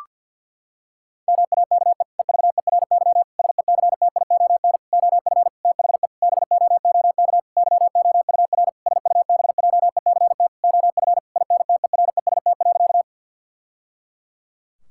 Morse code, heard as a keyed tone, about 50 wpm, 700 Hz. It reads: MAKE EVERY SECTION OF THE BOOK YOUR SUBJECT OF INTEREST.
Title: Sentences from Top 1000 Words - Review Part 5 - 50wpm